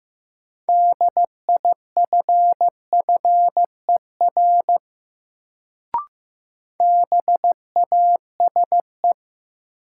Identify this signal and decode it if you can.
Morse code, heard as a keyed tone, about 15 wpm, 700 Hz. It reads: DIFFER BASE
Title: Sentences from Top 1000 Words - Review Part 1 - New Word Drill - 15wpm